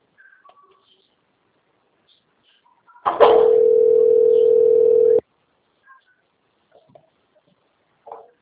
Ringback tone (North America), 440 + 480 Hz.